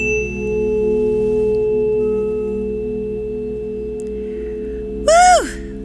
Dial tone (Europe, 425 Hz), with noise.